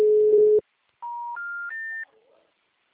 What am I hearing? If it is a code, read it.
Special information tone.
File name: out- -2043- - - .16659